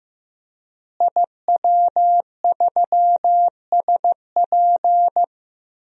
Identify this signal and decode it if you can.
Morse code, heard as a keyed tone, about 16 wpm, 700 Hz. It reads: IW3SP
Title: DXCC-100 Call Signs - 5 Characters - Repeated 2x - 15wpm